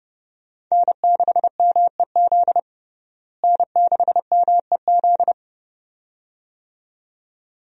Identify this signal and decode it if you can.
Morse code, heard as a keyed tone, about 30 wpm, 700 Hz. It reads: N6MEZ N6MEZ
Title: USA Call Signs - 5 Characters - Repeated 2x - 30wpm